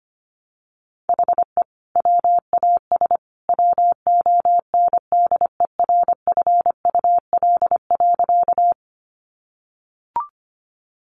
Morse code, heard as a keyed tone, about 25 wpm, 700 Hz. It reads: HE WAS WONDERFUL.
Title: Sentences from Top 1100 Words 25wpm